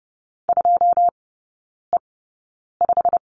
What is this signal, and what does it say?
Morse code, heard as a keyed tone, about 30 wpm, spaced as for 9 wpm, 700 Hz. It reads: 2E5
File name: Lesson 15e - Number 5 - Three Characters 30wpm